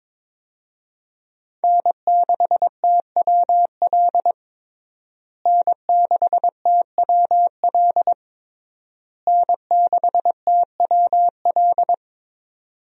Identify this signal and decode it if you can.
Morse code, heard as a keyed tone, about 22 wpm, 700 Hz. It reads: N6TWL N6TWL N6TWL
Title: USA Call Signs - 5 Characters - Repeated 3x - 22wpm